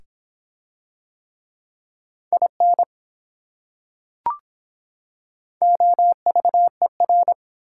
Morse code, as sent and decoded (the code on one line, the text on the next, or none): .. -. / --- ...- . .-.
IN OVER